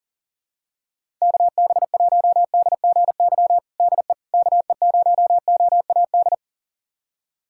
Morse code, heard as a keyed tone, about 40 wpm, 700 Hz. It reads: KB1DGY DE KE0OAD
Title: USA Callsign DE Callsign - 6 Characters - 40wpm